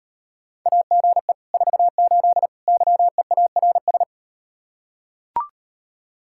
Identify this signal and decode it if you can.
Morse code, heard as a keyed tone, about 38 wpm, 700 Hz. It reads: AGE 48 YEARS